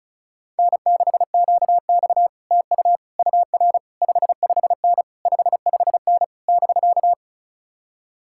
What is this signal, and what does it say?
Morse code, heard as a keyed tone, about 47 wpm, 700 Hz. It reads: N6QX TU UR 55N 55N <BK>